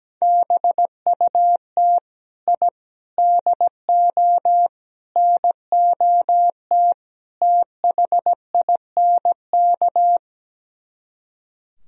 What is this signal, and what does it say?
Morse code, heard as a keyed tone, about 17 wpm, 700 Hz. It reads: BUT I DO NOT THINK
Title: Speed-Racing - Sets of 5 Words 17wpm